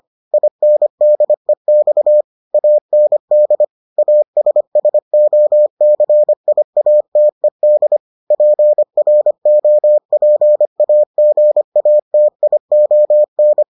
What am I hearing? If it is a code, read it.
Morse code, heard as a keyed tone, about 25 wpm, 600 Hz. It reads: INDEX AND ASSOCIATED PROPAGATION